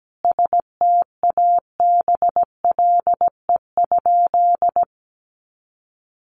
Morse code, heard as a keyed tone, about 17 wpm, 700 Hz. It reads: STABLE?
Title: Sentences from Top 1500 Words 17wpm